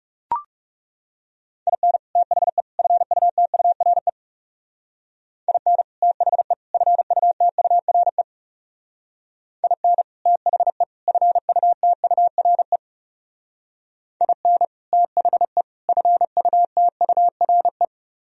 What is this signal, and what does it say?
Morse code, heard as a keyed tone, about 38 wpm, 700 Hz. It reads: IN THE FUTURE IN THE FUTURE IN THE FUTURE IN THE FUTURE